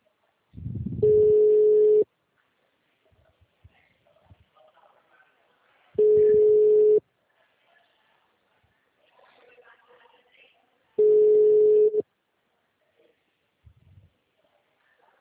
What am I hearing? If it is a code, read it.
Ringback tone (Europe, 425 Hz), with speech.